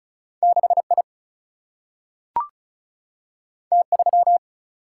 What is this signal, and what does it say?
Morse code, heard as a keyed tone, about 36 wpm, 700 Hz. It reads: 6 I T 3
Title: Single-Letter Single-Number 35wpm